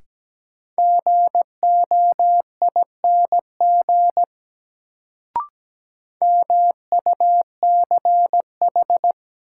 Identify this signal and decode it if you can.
Morse code, heard as a keyed tone, about 17 wpm, 700 Hz. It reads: GOING MUCH
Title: Sets of 2 Words - New Word Drill - 17wpm